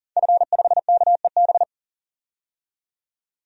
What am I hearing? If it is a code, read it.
Morse code, heard as a keyed tone, about 40 wpm, 700 Hz. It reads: F5KEB
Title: DXCC-100 Call Signs - 5 Characters - 40wpm